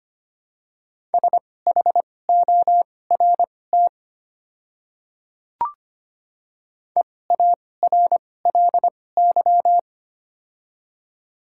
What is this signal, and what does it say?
Morse code, heard as a keyed tone, about 25 wpm, spaced as for 18 wpm, 700 Hz. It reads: SHORT EARLY
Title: 2x Character Spacing - Top 500 Words - 25wpm